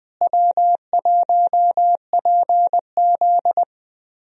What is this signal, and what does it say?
Morse code, heard as a keyed tone, about 21 wpm, 700 Hz. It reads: W1PZ
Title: USA Call Signs - 4 Characters - Repeated 2x - 20wpm